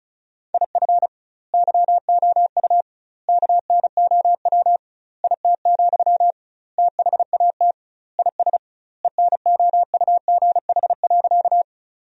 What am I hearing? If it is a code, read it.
Morse code, heard as a keyed tone, about 35 wpm, 700 Hz. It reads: IF YOU KNOW IT, THAT IS ENOUGH.
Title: 2x Word-Spacing - Sentences from Top 400 Words - 35wpm